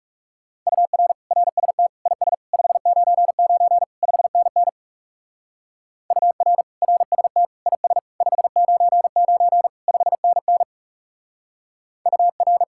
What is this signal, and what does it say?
Morse code, heard as a keyed tone, about 35 wpm, 700 Hz. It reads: UR RST IS 599 5NN UR RST IS 599 5NN UR
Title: Speed-Racing Common QSO Phrases - 30wpm